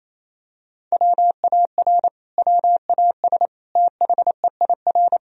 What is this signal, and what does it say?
Morse code, heard as a keyed tone, about 28 wpm, 700 Hz. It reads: WAR WAS THEIR
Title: Sentences from Top 1000 Words - Review Part 3 - 28wpm